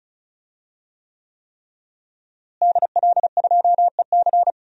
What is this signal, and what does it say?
Morse code, heard as a keyed tone, about 34 wpm, 700 Hz. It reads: DL2EC
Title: DXCC-100 Call Signs - 5 Characters - Repeated 3x - 35wpm